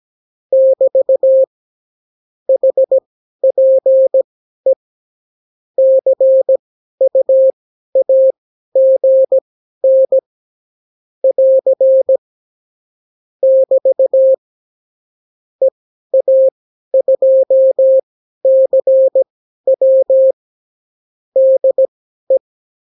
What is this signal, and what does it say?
Morse code, heard as a keyed tone, about 17 wpm, spaced as for 12 wpm, 530 Hz. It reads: <BT> HPE CUAGN <AR> <BT> EA2CW DE
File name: CW QSO libroCW4 18-8ppm